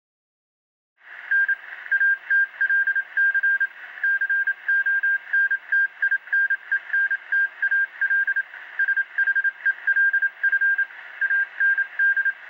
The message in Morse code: N ATLC CONTINENTAL SHELF AND